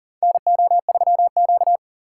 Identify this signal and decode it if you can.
Morse code, heard as a keyed tone, about 40 wpm, 700 Hz. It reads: NO3Q
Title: USA Call Signs - 4 Characters - 4x Character Spacing - 40wpm